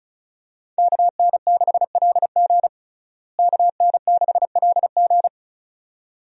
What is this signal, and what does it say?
Morse code, heard as a keyed tone, about 36 wpm, 700 Hz. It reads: KN6LG KN6LG
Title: USA Call Signs - 5 Characters - Repeated 2x - 35wpm